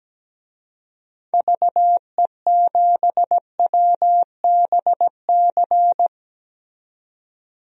Morse code, heard as a keyed tone, about 17 wpm, 700 Hz. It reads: VE7WBC
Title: DXCC-100 Call Signs - 6 Characters - 17wpm